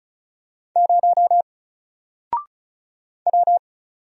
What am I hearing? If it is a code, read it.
Morse code, heard as a keyed tone, about 35 wpm, 700 Hz. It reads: 0 W
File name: Single-Letter-Number-35wpm